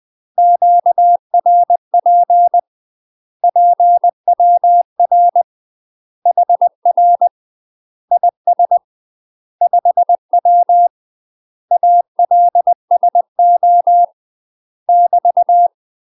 Morse code, heard as a keyed tone, about 20 wpm, 700 Hz. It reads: QRP PWR HR IS 5W ALSO <BT>
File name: UK MASTER QSOs TRAINING / 20WPM / UK MASTER QSOs 0043